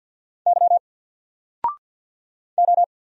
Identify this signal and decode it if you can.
Morse code, heard as a keyed tone, about 48 wpm, 700 Hz. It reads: Y O
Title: Single Letter-Number Alt 50wpm